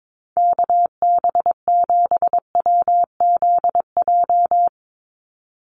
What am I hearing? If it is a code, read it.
Morse code, heard as a keyed tone, about 22 wpm, 700 Hz. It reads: KB7WZJ
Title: USA Call Signs 22wpm